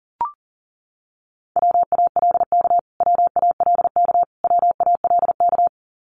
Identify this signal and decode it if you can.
Morse code, heard as a keyed tone, about 40 wpm, 700 Hz. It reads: WALK WALK WALK
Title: Top 300 Words Repeated 40wpm